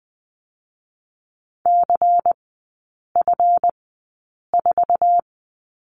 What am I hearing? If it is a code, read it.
Morse code, heard as a keyed tone, about 20 wpm, spaced as for 8 wpm, 700 Hz. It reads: CF4
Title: 138 Lesson 23e - Letter f - Three Characters 20wpm